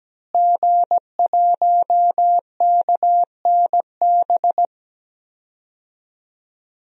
Morse code, heard as a keyed tone, about 17 wpm, 700 Hz. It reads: G1KNB
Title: DXCC-100 Call Signs - 5 Characters - Repeated 2x - 17wpm